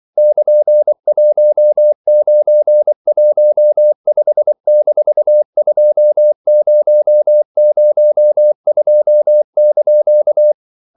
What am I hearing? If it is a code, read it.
Morse code, heard as a keyed tone, about 24 wpm, 600 Hz. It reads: <KN>1915-2002)
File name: CW / Practice / EnglishQuotations0424